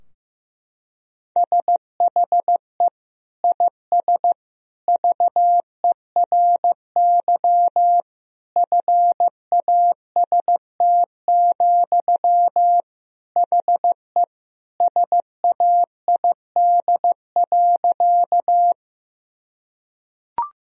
Morse code, heard as a keyed tone, about 15 wpm, 700 Hz. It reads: SHE IS VERY FAST, HE SAID.
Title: Sentences from Top 1000 Words - Review Part 2 - 15wpm